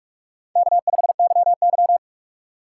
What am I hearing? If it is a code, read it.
Morse code, heard as a keyed tone, about 46 wpm, 700 Hz. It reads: K5YY
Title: USA Call Signs - 2x Character Spacing - 45wpm